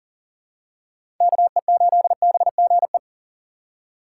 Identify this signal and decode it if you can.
Morse code, heard as a keyed tone, about 40 wpm, 700 Hz. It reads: KE8BGE
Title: USA Call Signs - 6 Characters - 4x Character Spacing - 40wpm